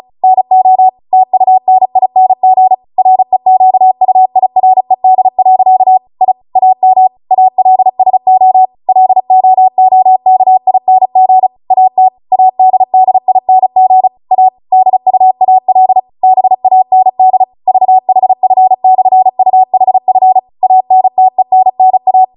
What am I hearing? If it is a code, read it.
Morse code, heard as a keyed tone, about 36 wpm, 750 Hz. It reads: NO TUNING REQUIRED. I AM ALSO LOOKING AT ADDING A DUAL BAND VHF/UHF ANTENNA